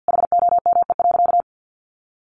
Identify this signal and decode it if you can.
Morse code, heard as a keyed tone, about 50 wpm, 700 Hz. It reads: HOME.
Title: Sentences from Top 500 Words 50wpm